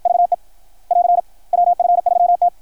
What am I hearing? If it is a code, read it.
Morse code, heard as a keyed tone, about 53 wpm, 690 Hz. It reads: VE <AR> WF3T
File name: archived / CRSnet